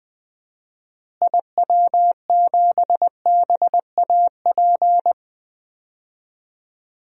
Morse code, heard as a keyed tone, about 20 wpm, 700 Hz. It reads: IW7BAP
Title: DXCC-100 Call Signs - 6 Characters - Repeated 3x - 20wpm